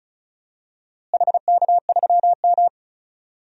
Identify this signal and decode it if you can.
Morse code, heard as a keyed tone, about 36 wpm, 700 Hz. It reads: HK3M
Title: DXCC-100 Call Signs - 4 Characters - Repeated 3x - 35wpm